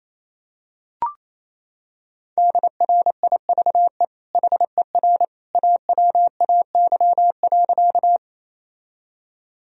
Morse code, heard as a keyed tone, about 28 wpm, 700 Hz. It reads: DRIVE HER AWAY.
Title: Sentences from Top 1000 Words - Review Part 2 - 28wpm